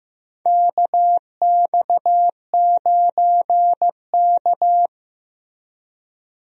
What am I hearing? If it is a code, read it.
Morse code, heard as a keyed tone, about 15 wpm, 700 Hz. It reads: KX9K
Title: USA Call Signs - 4 Characters - Repeated 2x - 15wpm